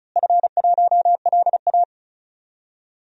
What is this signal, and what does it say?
Morse code, heard as a keyed tone, about 36 wpm, 700 Hz. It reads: F1LA
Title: DXCC-100 Call Signs - 4 Characters - Repeated 3x - 35wpm